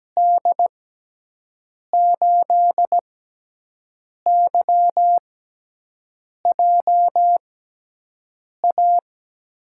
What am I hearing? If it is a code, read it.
Morse code, heard as a keyed tone, about 17 wpm, spaced as for 6 wpm, 700 Hz. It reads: D8YJA
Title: USA Call Signs - 6 Characters - 6x Character Spacing - 17wpm